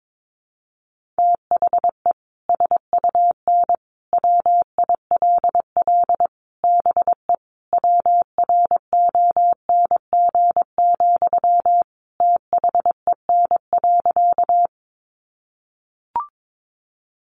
Morse code, heard as a keyed tone, about 22 wpm, 700 Hz. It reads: THE SUN WILL BE WRONG, THEN.
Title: Sentences from Top 900 Words 22wpm